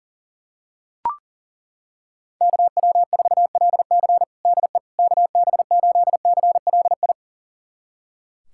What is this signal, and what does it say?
Morse code, heard as a keyed tone, about 40 wpm, 700 Hz. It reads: KW4LC DE KB8CLI
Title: USA Callsign DE Callsign 40wpm